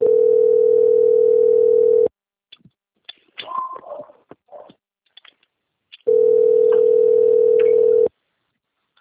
Ringback tone (North America), 440 + 480 Hz.